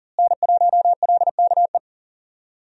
Morse code, heard as a keyed tone, about 40 wpm, 700 Hz. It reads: N1LKE
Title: USA Call Signs - 5 Characters - Repeated 2x - 40wpm